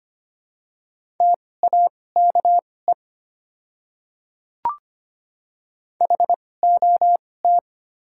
Morse code, heard as a keyed tone, about 25 wpm, spaced as for 18 wpm, 700 Hz. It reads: TAKE HOT
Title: 2x Character Spacing - Top 200 Words - 25wpm